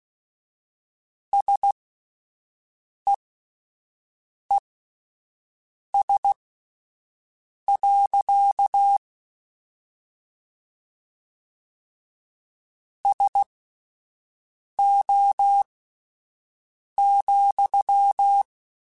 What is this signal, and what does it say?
Morse code, heard as a keyed tone, about 16 wpm, spaced as for 5 wpm, 800 Hz. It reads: SEES. SO,
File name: QSOs / QSOs-mp3s / WD0FBT 1@10